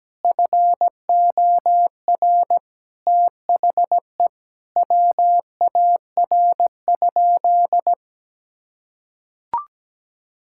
Morse code, heard as a keyed tone, about 17 wpm, 700 Hz. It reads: FOR THE WAR?
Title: Sentences from Top 1000 Words - Review Part 5 - 17wpm